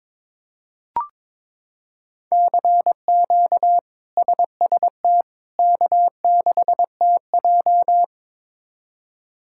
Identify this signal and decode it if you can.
Morse code, heard as a keyed tone, about 22 wpm, 700 Hz. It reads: CQ SST K6TJ
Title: K1USN SST QSO Practice - 22wpm